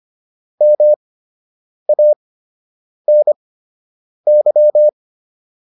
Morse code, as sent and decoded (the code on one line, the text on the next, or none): -- .- -. -.--
MANY